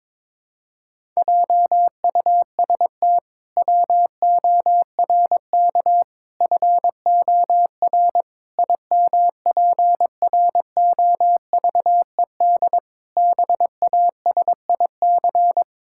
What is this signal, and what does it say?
Morse code, heard as a keyed tone, about 22 wpm, 700 Hz. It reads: JUST WORK FOR IMPROVED BASIC
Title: Pangram Challenge v2 - 22wpm